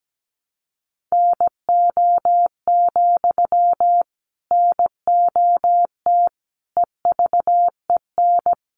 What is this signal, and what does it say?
Morse code, heard as a keyed tone, about 17 wpm, 700 Hz. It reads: NO, NOT EVEN
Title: Sentences from Top 1300 Words 17wpm